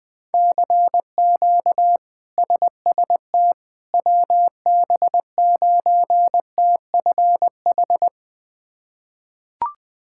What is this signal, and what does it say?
Morse code, heard as a keyed tone, about 20 wpm, 700 Hz. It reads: CQ SST WB9TFH
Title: K1USN SST QSO Practice - 20wpm